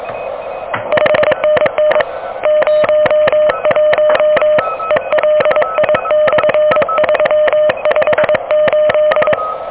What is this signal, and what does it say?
Morse code, heard as a keyed tone, about 24 wpm, 590 Hz, with noise. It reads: 5NN 01 ELI/358
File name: OK QRP 2023 / OK QRP 2023